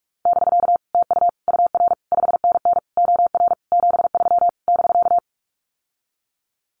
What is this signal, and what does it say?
Morse code, heard as a keyed tone, about 44 wpm, 700 Hz. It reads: <BK> TU UR 5NN OR 73 <BK>
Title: Common QSO Phrases Encore 45wpm